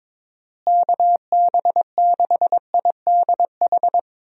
Morse code, heard as a keyed tone, about 22 wpm, 700 Hz. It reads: KB6IDH
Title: USA Call Signs - 6 Characters - Repeated 2x - 22wpm